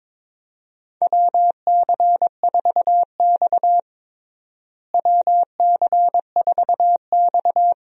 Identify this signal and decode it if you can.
Morse code, heard as a keyed tone, about 21 wpm, 700 Hz. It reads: WC4X WC4X
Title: USA Call Signs - 4 Characters - Repeated 2x - 22wpm